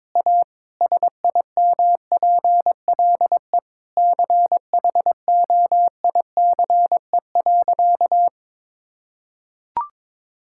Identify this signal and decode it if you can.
Morse code, heard as a keyed tone, about 22 wpm, 700 Hz. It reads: A SIMPLE CHOICE.
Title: Sentences from Top 1000 Words - Review Part 6 - 22wpm